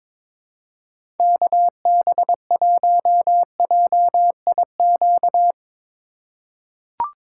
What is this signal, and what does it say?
Morse code, heard as a keyed tone, about 22 wpm, 700 Hz. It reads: KB1JIQ
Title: USA Call Signs - 6 Characters - 2x Character Spacing - 22wpm